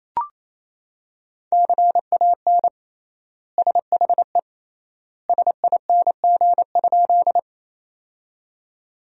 Morse code, heard as a keyed tone, about 28 wpm, 700 Hz. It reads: CAN SHE SING?
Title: 3x Word-Spacing - Sentences from Top 400 Words - 28wpm